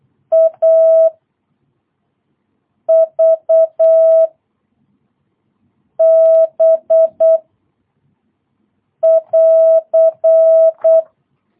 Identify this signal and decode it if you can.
Morse code, heard as a keyed tone, about 8 wpm, 650 Hz. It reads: A V B <AR>
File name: jh8cbh / 2022cw